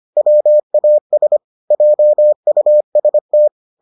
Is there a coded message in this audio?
Yes: Morse code from 0.2 to 3.5 s, heard as a keyed tone, about 25 wpm, 600 Hz. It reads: WAS JUST